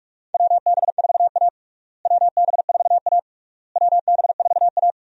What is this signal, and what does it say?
Morse code, heard as a keyed tone, about 41 wpm, 700 Hz. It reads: WB4A WB4A WB4A
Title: USA Call Signs - 4 Characters - Repeated 3x - 45wpm